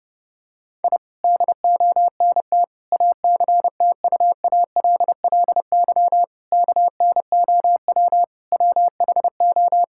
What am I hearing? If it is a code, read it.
Morse code, heard as a keyed tone, about 30 wpm, 700 Hz. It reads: I DONT ACTUALLY KNOW WHO
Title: Sentences from Top 1000 Words - Review Part 5 - 30wpm